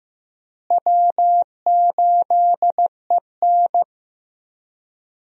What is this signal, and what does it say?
Morse code, heard as a keyed tone, about 15 wpm, 700 Hz. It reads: W8EN
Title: USA Call Signs - 4 Characters - Repeated 3x - 15wpm